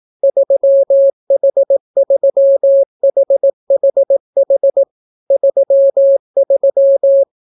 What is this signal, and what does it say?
Morse code, heard as a keyed tone, about 18 wpm, 550 Hz. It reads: 3H3HHH 33